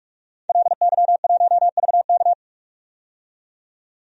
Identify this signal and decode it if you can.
Morse code, heard as a keyed tone, about 44 wpm, 690 Hz. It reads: LY1VK
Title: DXCC-100 Call Signs - 5 Characters - 45wpm